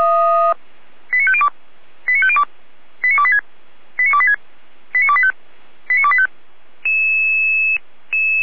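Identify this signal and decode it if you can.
Five-tone selective call (ZVEI), heard as quick runs of five tones, tones of about 70 ms. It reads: 89502 89502 89286 89286 89285 89285
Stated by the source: ZVEI five-tone call received by SDR.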